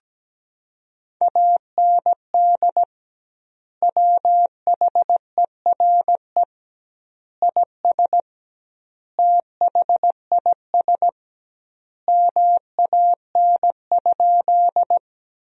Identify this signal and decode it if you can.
Morse code, heard as a keyed tone, about 17 wpm, 700 Hz. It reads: AND WHERE IS THIS MAN?
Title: 2x Word-Spacing - Sentences from Top 200 Words - 17wpm